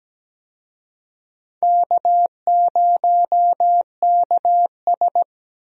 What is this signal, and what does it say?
Morse code, heard as a keyed tone, about 17 wpm, 700 Hz. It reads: K0KS
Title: USA Call Signs - 4 Characters - Repeated 2x - 17wpm